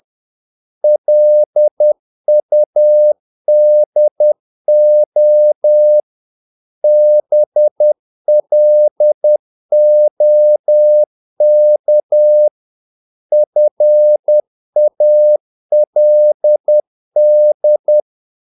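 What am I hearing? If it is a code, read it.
Morse code, heard as a keyed tone, about 10 wpm, 600 Hz. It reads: LUDO BLOK FALD
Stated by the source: Morse practice of short Danish words at 10 wpm, from CW med Gnister.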